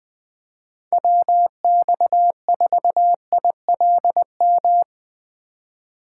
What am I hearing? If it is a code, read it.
Morse code, heard as a keyed tone, about 22 wpm, 700 Hz. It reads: WX4ILM
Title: Speed Racing - USA Call Signs 20wpm